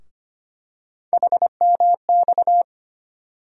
Morse code, heard as a keyed tone, about 25 wpm, 700 Hz. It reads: HMX
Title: USA Call Sign Postfixes - 3 Letters 25wpm